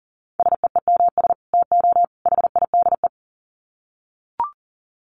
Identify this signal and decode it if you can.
Morse code, heard as a keyed tone, about 40 wpm, 700 Hz. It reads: SEEMS TO HIDE
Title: Dit Challenge 40wpm